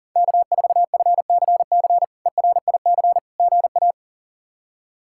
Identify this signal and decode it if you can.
Morse code, heard as a keyed tone, about 40 wpm, 700 Hz. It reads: K4FCC ERIC GA
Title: K1USN SST QSO Practice - 40wpm